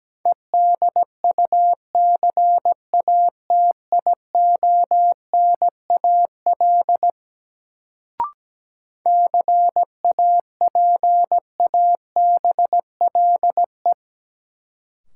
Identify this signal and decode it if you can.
Morse code, heard as a keyed tone, about 17 wpm, 700 Hz. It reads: EDUCATIONAL CAPABLE
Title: Sentences from Top 1200 Words - New Word Drill - 17wpm